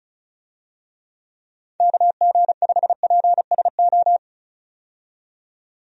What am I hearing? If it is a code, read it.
Morse code, heard as a keyed tone, about 34 wpm, 700 Hz. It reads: KG5PSO
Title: USA Call Signs - 6 Characters - 35wpm